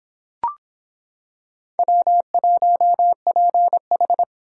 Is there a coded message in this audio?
Yes: Morse code from 1.8 to 4.2 s, heard as a keyed tone, about 25 wpm, 700 Hz. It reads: W1PH